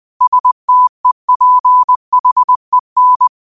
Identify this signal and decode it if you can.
Morse code, heard as a keyed tone, about 20 wpm, 1000 Hz. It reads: STEPHEN
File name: qotdfiles / 20wpm